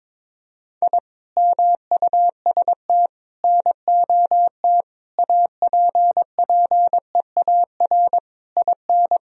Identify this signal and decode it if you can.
Morse code, heard as a keyed tone, about 22 wpm, 700 Hz. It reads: I MUST NOT APPEAR IN
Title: Sentences from Top 1000 Words - Review Part 2 - 22wpm